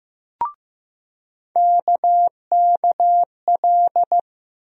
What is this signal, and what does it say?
Morse code, heard as a keyed tone, about 15 wpm, 700 Hz. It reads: KKL
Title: USA Call Sign Postfixes - 3 Letters 15wpm